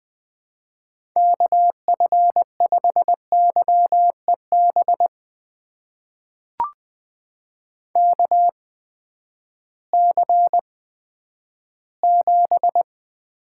Morse code, heard as a keyed tone, about 20 wpm, spaced as for 11 wpm, 700 Hz. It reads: KF5YEB KC7
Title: USA Call Signs - 6 Characters - 8x Character Spacing - 20wpm